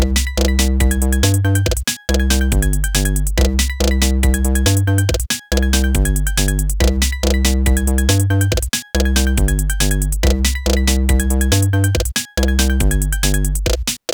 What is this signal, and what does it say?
There is no code.